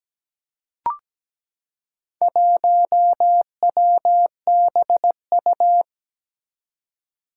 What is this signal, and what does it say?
Morse code, heard as a keyed tone, about 17 wpm, 700 Hz. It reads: 1WBU
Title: Four Challenging Letter-Number Combinations 17wpm